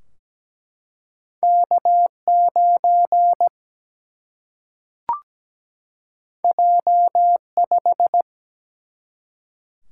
Morse code, heard as a keyed tone, about 17 wpm, 700 Hz. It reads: K9 J5
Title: Single-Letter Single-Number 17wpm